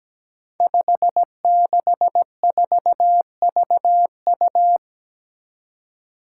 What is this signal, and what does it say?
Morse code, heard as a keyed tone, about 17 wpm, 700 Hz. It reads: 564VU
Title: Five Challenging Letter-Number Combinations 17wpm